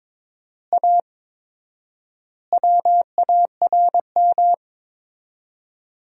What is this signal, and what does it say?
Morse code, heard as a keyed tone, about 22 wpm, 700 Hz. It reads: A WARM
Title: 4x Word-Spacing - Sentences from Top 500 Words - 22wpm